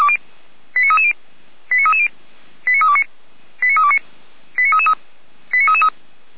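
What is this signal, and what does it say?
Five-tone selective call (ZVEI), heard as quick runs of five tones, tones of about 70 ms. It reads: 89330 89330 89329 89329 89333
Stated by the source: Received on an SDR.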